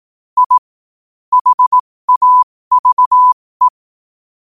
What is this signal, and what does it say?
Morse code, heard as a keyed tone, about 18 wpm, spaced as for 15 wpm, 1000 Hz. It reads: I HAVE
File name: qotdfiles / 15wpm